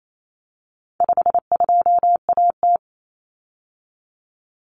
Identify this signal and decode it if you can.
Morse code, heard as a keyed tone, about 28 wpm, 700 Hz. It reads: 52AT
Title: Common Cut Numbers - 4-Digit 28wpm